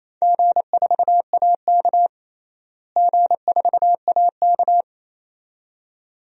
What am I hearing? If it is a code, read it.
Morse code, heard as a keyed tone, about 28 wpm, 700 Hz. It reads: G4AK G4AK
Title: DXCC-100 Call Signs - 4 Characters - Repeated 2x - 28wpm